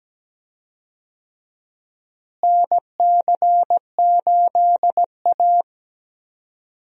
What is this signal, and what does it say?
Morse code, heard as a keyed tone, about 17 wpm, 700 Hz. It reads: NC8A
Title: USA Call Signs - 4 Characters - Repeated 3x - 17wpm